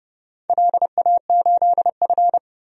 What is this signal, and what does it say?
Morse code, heard as a keyed tone, about 30 wpm, 700 Hz. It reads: LA8F
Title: DXCC-100 Call Signs - 4 Characters - 30wpm